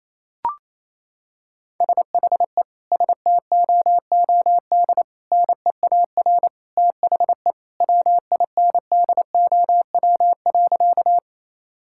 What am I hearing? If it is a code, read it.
Morse code, heard as a keyed tone, about 28 wpm, 700 Hz. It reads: SHE STOOD NEAR THE WINDOW.
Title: Sentences from Top 1600 Words 28wpm